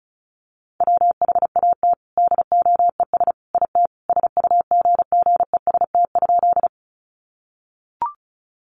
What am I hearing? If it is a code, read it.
Morse code, heard as a keyed tone, about 34 wpm, 700 Hz. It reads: WHAT DOES IT SUGGEST?
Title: Sentences from Top 1300 Words 35wpm